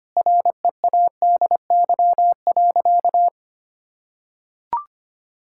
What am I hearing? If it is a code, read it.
Morse code, heard as a keyed tone, about 25 wpm, 700 Hz. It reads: READY.
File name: practice / 25wpm / Sentences with 3-Word Phrases - 25wpm